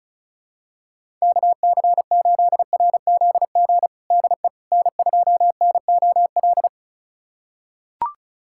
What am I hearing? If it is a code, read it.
Morse code, heard as a keyed tone, about 36 wpm, 700 Hz. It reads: KC8RZG DE N2NOL
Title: USA Callsign DE Callsign 35wpm